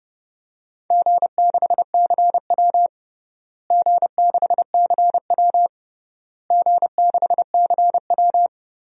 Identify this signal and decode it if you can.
Morse code, heard as a keyed tone, about 30 wpm, 700 Hz. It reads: G6CW G6CW G6CW
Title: DXCC-100 Call Signs - 4 Characters - Repeated 3x - 30wpm